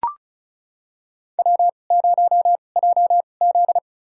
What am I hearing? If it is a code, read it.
Morse code, heard as a keyed tone, about 34 wpm, spaced as for 25 wpm, 700 Hz. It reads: W0JZ
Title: USA Call Signs - 4 Characters - 2x Character Spacing - 35wpm